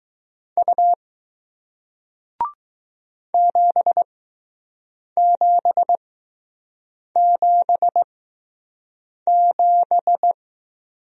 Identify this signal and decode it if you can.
Morse code, heard as a keyed tone, about 20 wpm, spaced as for 6 wpm, 700 Hz. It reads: U 7777